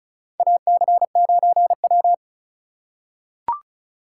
Morse code, heard as a keyed tone, about 34 wpm, 690 Hz. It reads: AC9W
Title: USA Call Signs - 3x Character Spacing - 35wpm